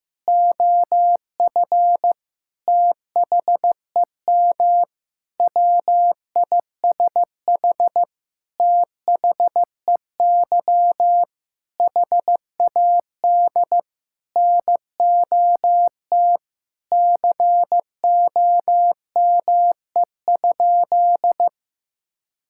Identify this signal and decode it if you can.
Morse code, heard as a keyed tone, about 15 wpm, 700 Hz. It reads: OF THEM WISH THEY HAD NOT COME?
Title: Sentences from Top 1600 Words 15wpm